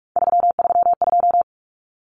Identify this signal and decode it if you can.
Morse code, heard as a keyed tone, about 44 wpm, 700 Hz. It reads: 332
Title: Common Cut Numbers - 3-Digit 45wpm